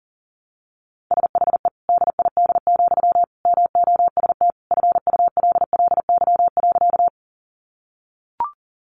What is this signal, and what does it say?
Morse code, heard as a keyed tone, about 40 wpm, 700 Hz. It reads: SHE DID, MOST FULLY.
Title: Sentences from Top 1300 Words 40wpm